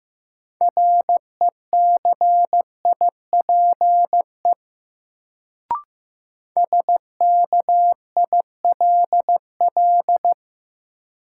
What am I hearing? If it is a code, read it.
Morse code, heard as a keyed tone, about 15 wpm, 700 Hz. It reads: RECIPE SKILL
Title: Sentences from Top 1200 Words - New Word Drill Alt - 15wpm